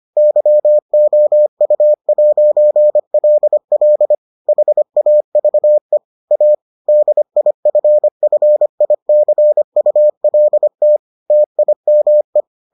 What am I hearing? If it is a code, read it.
Morse code, heard as a keyed tone, about 25 wpm, 600 Hz. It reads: YOU'LL HAVE A DIFFICULT TIME